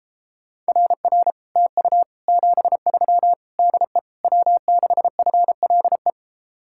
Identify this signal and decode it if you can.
Morse code, heard as a keyed tone, about 33 wpm, 700 Hz. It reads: RR TU 73 DE W6FLE